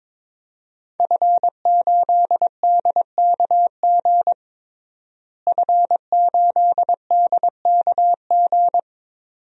Morse code, heard as a keyed tone, about 22 wpm, 700 Hz. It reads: F8DKG F8DKG
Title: DXCC-100 Call Signs - 5 Characters - Repeated 2x - 22wpm